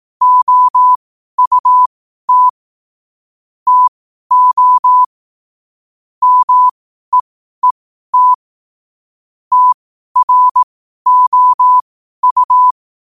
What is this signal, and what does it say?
Morse code, heard as a keyed tone, about 18 wpm, spaced as for 12 wpm, 1000 Hz. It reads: OUT TO MEET TROU
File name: qotdfiles / 12wpm